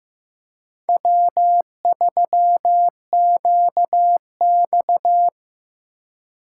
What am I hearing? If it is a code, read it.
Morse code, heard as a keyed tone, about 15 wpm, 700 Hz. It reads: W3QX
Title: USA Call Signs - 4 Characters - 2x Character Spacing - 15wpm